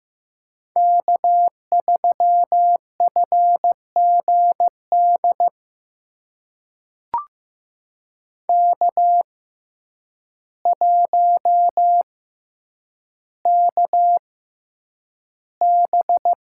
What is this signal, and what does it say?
Morse code, heard as a keyed tone, about 15 wpm, spaced as for 8 wpm, 700 Hz. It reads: K3FGD K1KB
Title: USA Call Signs - 6x Character Spacing - 15wpm